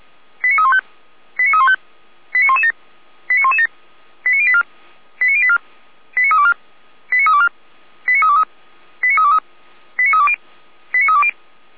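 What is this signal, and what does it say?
Five-tone selective call (ZVEI), heard as quick runs of five tones, tones of about 70 ms. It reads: 89315 89315 89117 89117 89084 89084 89324 89324 89323 89323 89320 89320
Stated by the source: ZVEI five-tone selective call received via SDR, recorded by Feuerwehr Orlamünde.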